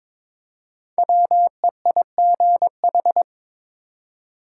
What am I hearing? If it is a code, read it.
Morse code, heard as a keyed tone, about 23 wpm, 700 Hz. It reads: WEIGH
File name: 5 Letter Words - First 3 Letters Given - 22wpm